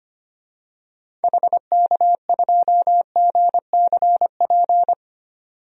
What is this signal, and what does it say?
Morse code, heard as a keyed tone, about 25 wpm, 700 Hz. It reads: HK2GCP
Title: DXCC-100 Call Signs - 6 Characters - 25wpm